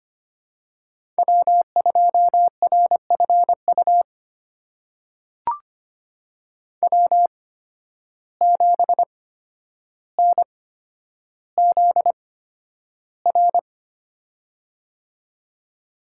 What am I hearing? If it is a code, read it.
Morse code, heard as a keyed tone, about 25 wpm, spaced as for 11 wpm, 700 Hz. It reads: W2RFU W7NZR
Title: USA Call Signs - 5 Characters - 8x Character Spacing - 25wpm